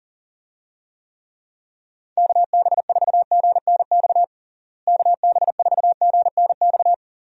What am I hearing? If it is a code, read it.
Morse code, heard as a keyed tone, about 40 wpm, 700 Hz. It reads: KB4GNX KB4GNX
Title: USA Call Signs - 6 Characters - Repeated 2x - 40wpm